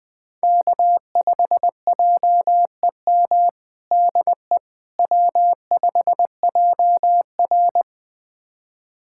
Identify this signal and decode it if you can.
Morse code, heard as a keyed tone, about 20 wpm, 700 Hz. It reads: K5JEM DE W5JR
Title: USA Callsign DE Callsign 20wpm